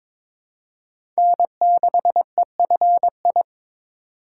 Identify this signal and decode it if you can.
Morse code, heard as a keyed tone, about 22 wpm, 700 Hz. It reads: N6EFI
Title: USA Call Signs - 5 Characters - 2x Character Spacing - 22wpm